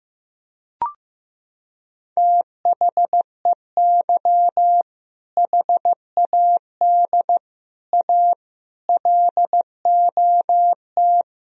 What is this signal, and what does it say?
Morse code, heard as a keyed tone, about 15 wpm, 700 Hz. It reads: THEY HAD A LOT
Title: Sentences from Top 1000 Words - Review Part 3 - 15wpm